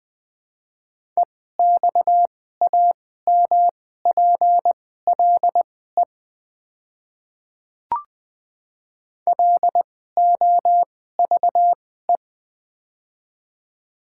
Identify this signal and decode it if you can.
Morse code, heard as a keyed tone, about 20 wpm, spaced as for 14 wpm, 700 Hz. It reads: EXAMPLE LOVE
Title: 2x Character Spacing - Top 500 Words - 20wpm